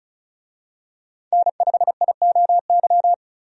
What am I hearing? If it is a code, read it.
Morse code, heard as a keyed tone, about 36 wpm, 700 Hz. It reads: N5IOY
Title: USA Call Signs - 5 Characters - 35wpm